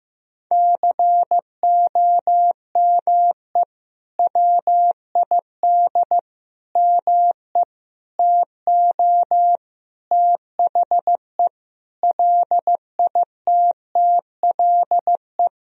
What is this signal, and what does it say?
Morse code, heard as a keyed tone, about 16 wpm, 700 Hz. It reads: COME WID ME TO THE LITTLE
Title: Sentences from Top 200 Words and Abbreviations 15wpm